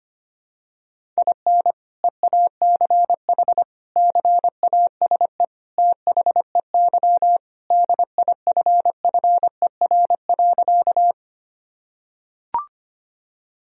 Morse code, heard as a keyed tone, about 25 wpm, 700 Hz. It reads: IN EACH CASE THEY DIFFER.
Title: Sentences from Top 1000 Words - Review Part 1 - 25wpm